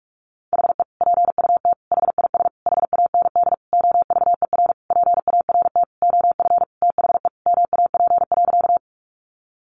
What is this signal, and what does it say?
Morse code, heard as a keyed tone, about 44 wpm, 700 Hz. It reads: HE PUT HIS HAND OVER PART OF THE MAP.
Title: Sentences from Top 400 Words 45wpm